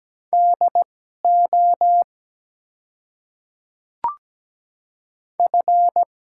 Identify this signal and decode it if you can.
Morse code, heard as a keyed tone, about 17 wpm, spaced as for 12 wpm, 700 Hz. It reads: DO F
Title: 2x Character Spacing - Top 300 Words - 17wpm